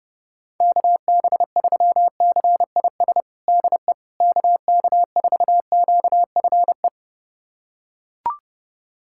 Morse code, heard as a keyed tone, about 30 wpm, 700 Hz. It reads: KB3CIS DE KK4QFE
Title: USA Callsign DE Callsign - 6 Characters - 30wpm